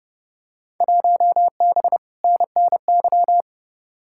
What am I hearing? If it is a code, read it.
Morse code, heard as a keyed tone, about 30 wpm, 700 Hz. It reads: 1B NNY
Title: Field Day Exchanges - 30wpm